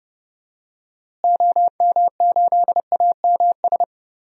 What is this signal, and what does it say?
Morse code, heard as a keyed tone, about 30 wpm, 700 Hz. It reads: OM8AMS
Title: DXCC-100 Call Signs - 6 Characters - Repeated 3x - 30wpm